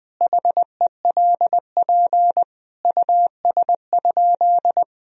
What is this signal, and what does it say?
Morse code, heard as a keyed tone, about 20 wpm, 700 Hz. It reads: HELP US?
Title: Sentences from Top 1000 Words - Review Part 3 - 20wpm